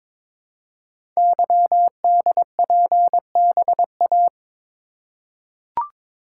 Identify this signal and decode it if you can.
Morse code, heard as a keyed tone, about 22 wpm, 700 Hz. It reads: YDPBA
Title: Code Groups - 5 Letters - 22wpm